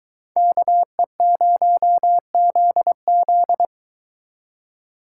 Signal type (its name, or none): Morse code